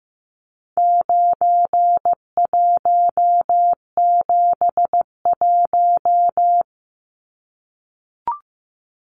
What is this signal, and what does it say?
Morse code, heard as a keyed tone, about 15 wpm, 700 Hz. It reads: 9171
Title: Numbers - 4-Digit 15wpm